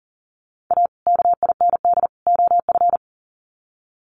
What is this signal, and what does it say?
Morse code, heard as a keyed tone, about 40 wpm, 700 Hz. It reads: A KIND OF
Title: Sets of 3 Words 40wpm